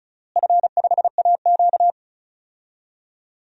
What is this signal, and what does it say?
Morse code, heard as a keyed tone, about 36 wpm, 700 Hz. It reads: F5AQ
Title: DXCC-100 Call Signs - 4 Characters - Repeated 2x - 35wpm